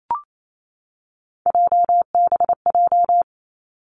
Morse code, heard as a keyed tone, about 28 wpm, 700 Hz. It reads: JBJ
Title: Three Challenging Letter-Number Combinations 28wpm